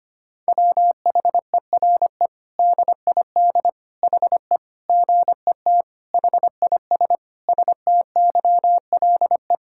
Morse code, heard as a keyed tone, about 25 wpm, 700 Hz. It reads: WHERE DID HE GET HIS STYLE
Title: Sentences from Top 1000 Words - Review Part 5 - 25wpm